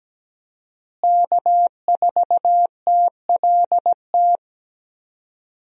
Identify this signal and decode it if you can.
Morse code, heard as a keyed tone, about 17 wpm, 700 Hz. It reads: K4TLT
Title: USA Call Signs - 5 Characters - Repeated 3x - 17wpm